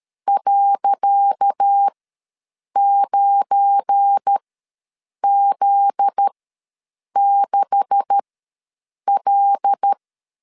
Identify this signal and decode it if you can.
Morse code, heard as a keyed tone, about 13 wpm, spaced as for 7 wpm, 800 Hz. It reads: .9Z6L
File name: LEZIONI / 10 SIGNS / 10C DOT ws 12wpm 3548